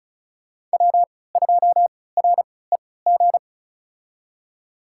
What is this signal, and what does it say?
Morse code, heard as a keyed tone, about 36 wpm, spaced as for 20 wpm, 700 Hz. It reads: W2REG
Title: USA Call Signs - 5 Characters - 3x Character Spacing - 35wpm